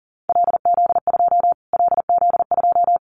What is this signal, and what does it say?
Morse code, heard as a keyed tone, about 37 wpm, 700 Hz. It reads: LZ2 LZ2
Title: Callsign Prefixes for Top 25 DXCC - Repeated 3x - 40wpm